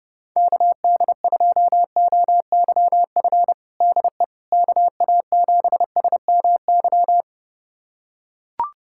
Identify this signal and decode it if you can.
Morse code, heard as a keyed tone, about 30 wpm, 700 Hz. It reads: KD2OYF DE KA7SMY
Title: USA Callsign DE Callsign - 6 Characters - 30wpm